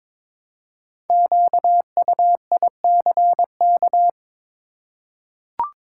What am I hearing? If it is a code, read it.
Morse code, heard as a keyed tone, about 22 wpm, 700 Hz. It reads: QUICK
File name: 5 Letter Words - First 3 Letters Given - 22wpm